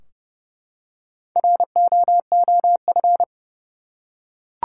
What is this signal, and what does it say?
Morse code, heard as a keyed tone, about 31 wpm, 700 Hz. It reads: ROOF
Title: Four Letter Words - 30wpm